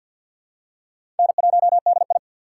Morse code, heard as a keyed tone, about 45 wpm, 700 Hz. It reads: N1DI
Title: USA Call Signs - 4 Characters - 3x Character Spacing - 50wpm